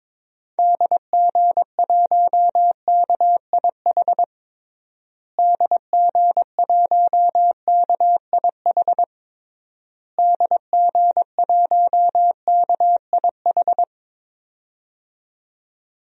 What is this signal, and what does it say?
Morse code, heard as a keyed tone, about 21 wpm, 700 Hz. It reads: DG1KIH DG1KIH DG1KIH
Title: DXCC-100 Call Signs - 6 Characters - Repeated 3x - 22wpm